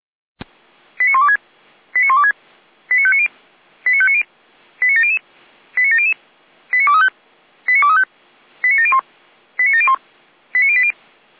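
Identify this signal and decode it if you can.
Five-tone selective call (ZVEI), heard as quick runs of five tones, tones of about 70 ms. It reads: 89126 89126 89590 89590 89700 89700 89235 89235 89701 89701 89080
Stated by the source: Received on an SDR, ZVEI five-tone call.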